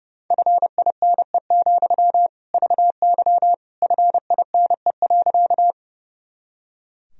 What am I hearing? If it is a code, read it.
Morse code, heard as a keyed tone, about 32 wpm, 700 Hz. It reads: FINE, VY FINE.
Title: Sentences from Top 500 Words and Abbreviations 30wpm